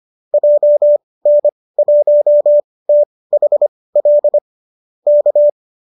Morse code, heard as a keyed tone, about 25 wpm, spaced as for 18 wpm, 590 Hz. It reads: JN1THL K
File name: qso205 18